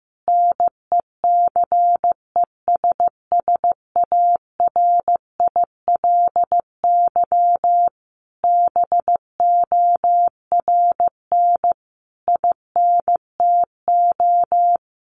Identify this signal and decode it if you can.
Morse code, heard as a keyed tone, about 15 wpm, 700 Hz. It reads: NECESSARILY BORN INTO
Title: Sentences from Top 1400 Words 15wpm